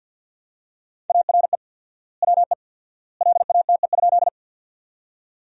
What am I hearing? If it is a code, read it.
Morse code, heard as a keyed tone, about 51 wpm, 700 Hz. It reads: ARE WE LATE?
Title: 4x Word-Spacing - Sentences from Top 300 Words - 50wpm